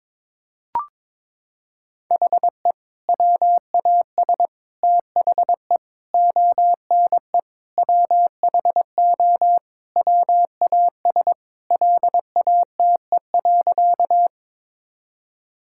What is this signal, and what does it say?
Morse code, heard as a keyed tone, about 22 wpm, 700 Hz. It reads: HE WAS THE ONE WHO WAS LATE.
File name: practice / 22wpm / Sentences with 2-Word Phrases - 22wpm